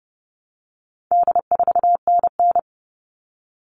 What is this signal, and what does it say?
Morse code, heard as a keyed tone, about 30 wpm, 700 Hz. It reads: D4NN
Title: All Cut Numbers - 4-Digit 30wpm